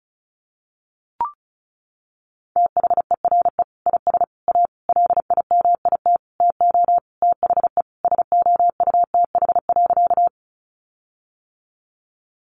Morse code, heard as a keyed tone, about 34 wpm, 700 Hz. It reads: THERE IS A LIMIT TO THE SOUTH.
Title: Sentences from Top 1500 Words 35wpm